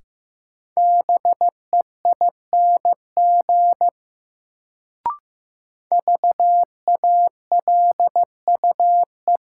Morse code, heard as a keyed tone, about 15 wpm, 700 Hz. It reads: BEING VALUE
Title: Sentences from Top 600 Words - New Word Drill - 15wpm